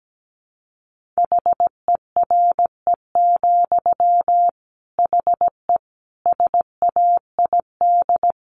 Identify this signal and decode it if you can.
Morse code, heard as a keyed tone, about 17 wpm, 700 Hz. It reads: HERE, HE SAID
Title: Sentences from Top 1500 Words 17wpm